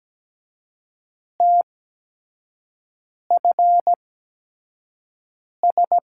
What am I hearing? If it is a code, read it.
Morse code, heard as a keyed tone, about 17 wpm, spaced as for 5 wpm, 700 Hz. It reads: TFS
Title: USA Call Signs - 5 Characters - 8x Character Spacing - 17wpm